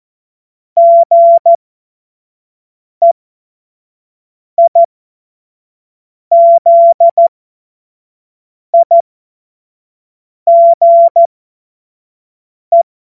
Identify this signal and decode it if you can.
Morse code, heard as a keyed tone, about 14 wpm, spaced as for 5 wpm, 680 Hz. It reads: GEIZIGE
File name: Witz-12-August-2025-25-F70